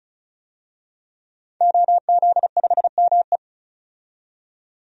Morse code, heard as a keyed tone, about 35 wpm, 700 Hz. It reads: OZ5ME
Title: DXCC-100 Call Signs - 5 Characters - 35wpm